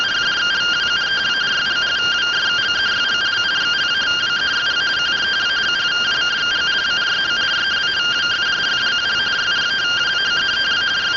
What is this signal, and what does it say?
Two-tone teleprinter signal, about 170 Hz shift.